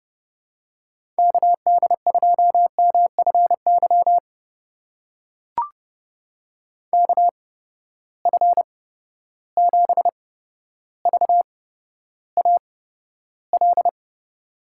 Morse code, heard as a keyed tone, about 30 wpm, spaced as for 13 wpm, 700 Hz. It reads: KD2MFY KF7VAL
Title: USA Call Signs - 6 Characters - 8x Character Spacing - 30wpm